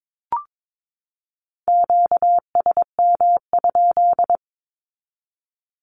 Morse code, heard as a keyed tone, about 22 wpm, 700 Hz. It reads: QSM?
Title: Q-Codes 22wpm